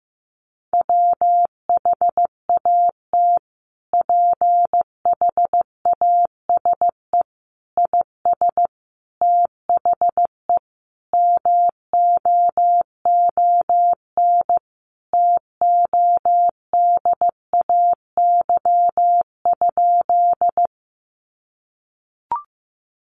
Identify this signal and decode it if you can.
Morse code, heard as a keyed tone, about 15 wpm, 700 Hz. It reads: WHAT PHASE IS THE MOON TODAY?
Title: Sentences from Top 1200 Words 15wpm